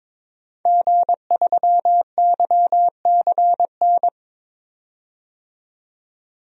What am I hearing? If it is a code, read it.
Morse code, heard as a keyed tone, about 22 wpm, 700 Hz. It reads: G3YCN
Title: DXCC-100 Call Signs - 5 Characters - 22wpm